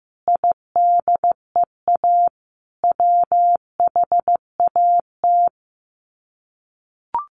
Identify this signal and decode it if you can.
Morse code, heard as a keyed tone, about 15 wpm, 700 Hz. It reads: IDEA WHAT
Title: Sets of 5 Words Encore 15wpm